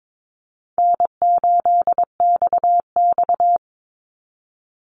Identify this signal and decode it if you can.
Morse code, heard as a keyed tone, about 22 wpm, 700 Hz. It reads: N8XX
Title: Call Signs - 22wpm